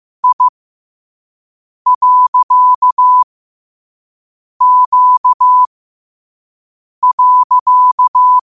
Morse code, heard as a keyed tone, about 15 wpm, spaced as for 5 wpm, 1000 Hz. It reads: I.Q.
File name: qotdfiles / 5wpm